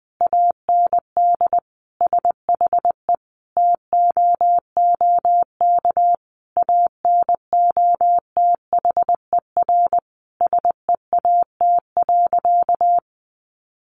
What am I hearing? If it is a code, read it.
Morse code, heard as a keyed tone, about 20 wpm, 700 Hz. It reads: AND SHE TOOK ANOTHER SEAT.
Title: Sentences from Top 1500 Words 20wpm